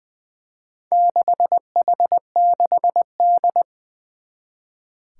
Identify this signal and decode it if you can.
Morse code, heard as a keyed tone, about 20 wpm, 700 Hz. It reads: 6H6D
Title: Four Challenging Letter-Number Combinations 20wpm